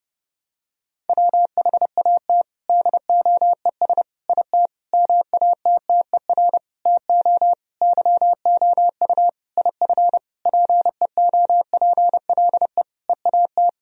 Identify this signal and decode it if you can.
Morse code, heard as a keyed tone, about 30 wpm, 700 Hz. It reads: WHAT DOES IT MATTER TO YOU IF PEOPLE EAT
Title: Sentences from Top 1000 Words - Review Part 3 - 30wpm